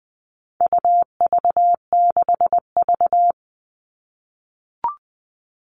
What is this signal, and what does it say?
Morse code, heard as a keyed tone, about 20 wpm, 700 Hz. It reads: UV6V
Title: All Cut Numbers - 4-Digit 20wpm